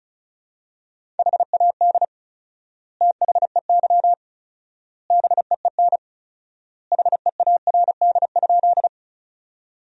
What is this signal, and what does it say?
Morse code, heard as a keyed tone, about 36 wpm, 700 Hz. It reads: HAD THEY BEEN HEARD?
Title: 4x Word-Spacing - Sentences from Top 400 Words - 35wpm